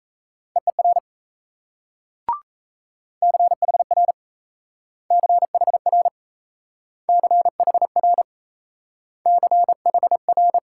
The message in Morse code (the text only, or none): EER CHR CHR CHR CHR